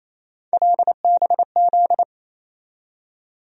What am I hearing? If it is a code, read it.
Morse code, heard as a keyed tone, about 28 wpm, 700 Hz. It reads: LBZ
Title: USA Call Sign Postfixes - 3 Letters 28wpm